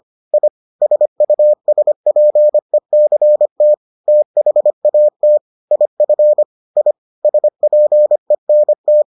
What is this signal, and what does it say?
Morse code, heard as a keyed tone, about 25 wpm, 600 Hz. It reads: I SUSPECT THAT IF I SPENT